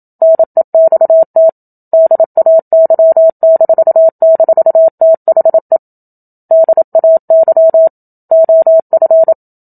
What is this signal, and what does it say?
Morse code, heard as a keyed tone, about 27 wpm, 650 Hz. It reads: NEXT DAY--THE DAY OF
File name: Book Two / 01-Chapter 1 - 27 WPM